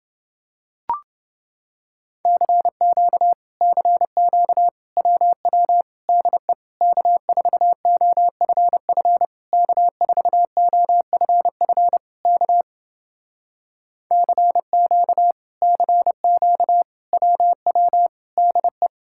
Morse code, heard as a keyed tone, about 30 wpm, 700 Hz. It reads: CQ CQ WW DE K4OFF K4OFF K CQ CQ WW DE